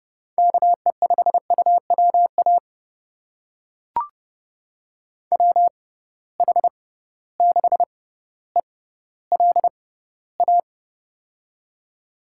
Morse code, heard as a keyed tone, about 30 wpm, spaced as for 15 wpm, 700 Hz. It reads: KE5UWA WH6ELA
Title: USA Call Signs - 6 Characters - 6x Character Spacing - 30wpm